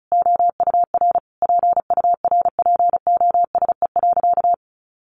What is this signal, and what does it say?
Morse code, heard as a keyed tone, about 34 wpm, 700 Hz. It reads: OUR PURPOSE.
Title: Sentences from Top 800 Words 35wpm